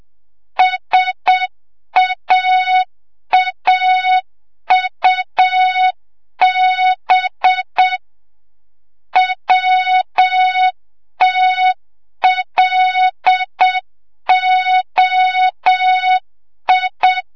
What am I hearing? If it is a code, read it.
Morse code, heard as a keyed tone, about 7 wpm, 3700 Hz, with noise. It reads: SAAUB WTLOI
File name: Morze-HA2MN / hang / 25 GYAKORLAT